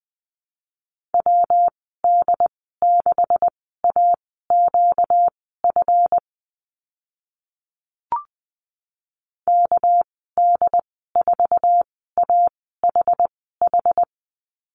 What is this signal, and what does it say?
Morse code, heard as a keyed tone, about 20 wpm, spaced as for 15 wpm, 700 Hz. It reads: WD6AQF KD4AHH
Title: 245 Lesson 39g - Letter z - Call Signs 20wpm